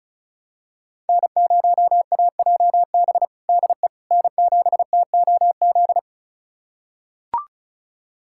Morse code, heard as a keyed tone, about 36 wpm, 700 Hz. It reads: N0AJB DE N7TOZ
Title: USA Callsign DE Callsign - 5 Characters - 35wpm